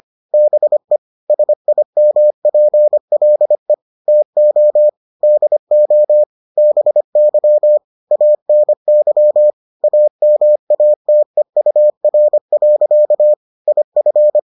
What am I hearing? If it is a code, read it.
Morse code, heard as a keyed tone, about 25 wpm, 600 Hz. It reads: BE SIMPLE TO DO BY ANY AMATEUR. IF